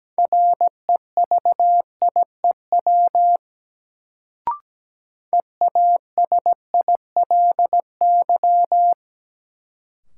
Morse code, heard as a keyed tone, about 17 wpm, 700 Hz. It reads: REVIEW EASILY
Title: Sentences from Top 900 Words - New Word Drill - 17wpm